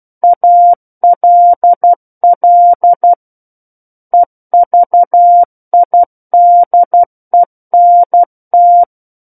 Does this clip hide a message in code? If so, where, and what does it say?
Yes: Morse code from 0.2 to 8.8 s, heard as a keyed tone, about 12 wpm, 700 Hz. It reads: ALL EVIDENT